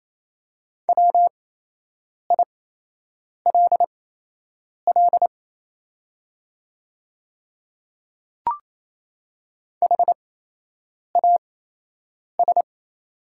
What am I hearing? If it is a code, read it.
Morse code, heard as a keyed tone, about 28 wpm, spaced as for 8 wpm, 700 Hz. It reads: WILL HAS
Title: 8x Character Spacing - Top 100 Words - 28wpm